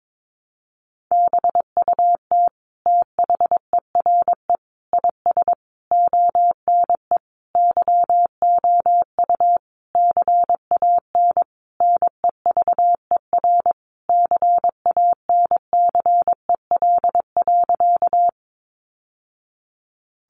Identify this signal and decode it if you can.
Morse code, heard as a keyed tone, about 22 wpm, 700 Hz. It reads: BUT THERE IS ONE YOU CAN NEVER CANCEL.
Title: Sentences from Top 1200 Words 22wpm